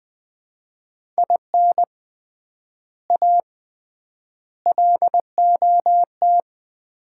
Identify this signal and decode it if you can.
Morse code, heard as a keyed tone, about 20 wpm, 700 Hz. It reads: IN A LOT
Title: 3x Word-Spacing - Sets of 4 Words - 20wpm